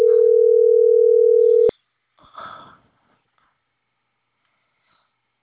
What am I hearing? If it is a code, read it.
Ringback tone (North America), 440 + 480 Hz.